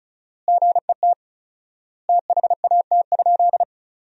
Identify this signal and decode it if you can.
Morse code, heard as a keyed tone, about 36 wpm, 700 Hz. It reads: GET THAT?
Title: 4x Word-Spacing - Sentences from Top 200 Words - 35wpm